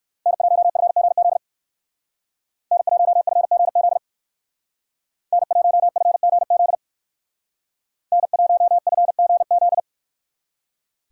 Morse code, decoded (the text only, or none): N1FGZ N1FGZ N1FGZ N1FGZ